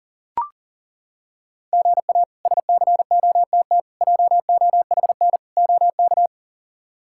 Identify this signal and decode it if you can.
Morse code, heard as a keyed tone, about 40 wpm, 700 Hz. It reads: GA SCOTT JOHN OK
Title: K1USN SST QSO Practice - 40wpm